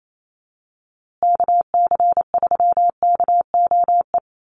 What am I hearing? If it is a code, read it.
Morse code, heard as a keyed tone, about 28 wpm, 700 Hz. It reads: KC3KOE
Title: USA Call Signs Repeated 3x 28wpm